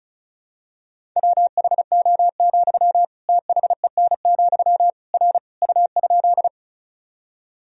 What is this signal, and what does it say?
Morse code, heard as a keyed tone, about 36 wpm, 700 Hz. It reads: WHO, THEN, R U?
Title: Sentences from Top 100 Words and Abbreviations 35wpm